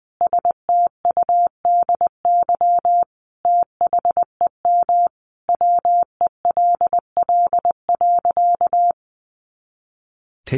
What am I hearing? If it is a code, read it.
Morse code, heard as a keyed tone, about 20 wpm, 700 Hz. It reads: STUDY THEM WELL.
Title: Sentences from Top 300 Words 20wpm